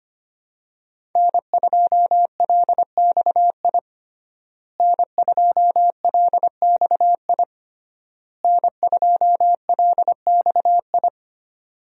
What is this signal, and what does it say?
Morse code, heard as a keyed tone, about 25 wpm, 700 Hz. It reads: N2LXI N2LXI N2LXI
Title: USA Call Signs - 5 Characters - Repeated 3x - 25wpm